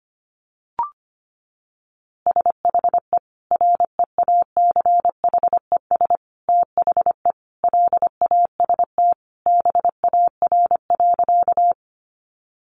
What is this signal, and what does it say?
Morse code, heard as a keyed tone, about 25 wpm, 700 Hz. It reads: SHE REACHES THE LAST BAR.
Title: Easy Phrases 25wpm